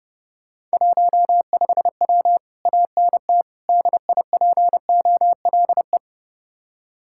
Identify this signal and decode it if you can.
Morse code, heard as a keyed tone, about 33 wpm, 700 Hz. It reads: 15W ANT DIPOLE